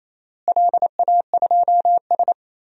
Morse code, heard as a keyed tone, about 28 wpm, 700 Hz. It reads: LA2S
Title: DXCC-100 Call Signs - 4 Characters - Repeated 3x - 28wpm